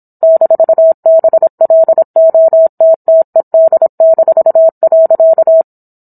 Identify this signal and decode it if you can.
Morse code, heard as a keyed tone, about 27 wpm, 650 Hz. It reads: -BLOTTED-.